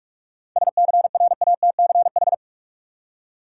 Morse code, heard as a keyed tone, about 46 wpm, 700 Hz. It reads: SCRATCH
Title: Seven Letter Words - 45wpm